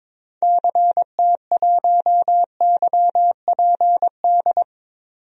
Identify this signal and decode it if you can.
Morse code, heard as a keyed tone, about 22 wpm, 700 Hz. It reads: CT1YPD
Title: DXCC-100 Call Signs - 6 Characters - 22wpm